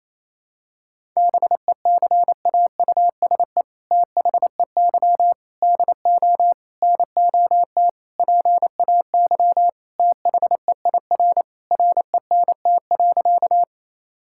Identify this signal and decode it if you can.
Morse code, heard as a keyed tone, about 28 wpm, 700 Hz. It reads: BECAUSE THEY DO NOT PAY THEIR RENT.
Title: Sentences from Top 1000 Words - Review Part 5 - 28wpm